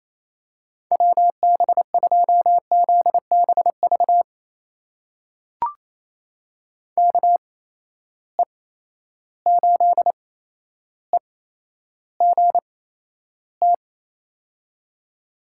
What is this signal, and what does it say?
Morse code, heard as a keyed tone, about 28 wpm, spaced as for 12 wpm, 700 Hz. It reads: WB2ZBV KE8EGT
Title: USA Call Signs - 8x Character Spacing - 28wpm